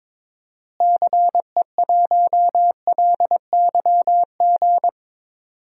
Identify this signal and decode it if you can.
Morse code, heard as a keyed tone, about 22 wpm, 700 Hz. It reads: CE1LYG
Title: DXCC-100 Call Signs - 6 Characters - Repeated 3x - 22wpm